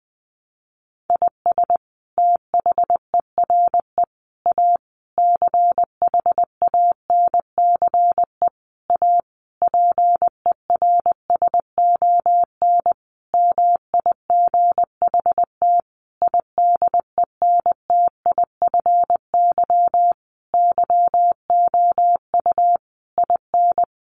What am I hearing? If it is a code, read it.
Morse code, heard as a keyed tone, about 20 wpm, 700 Hz. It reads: IS THERE A CHANCE A PERSON MIGHT IDENTIFY YOU IN
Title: Sentences from Top 900 Words 20wpm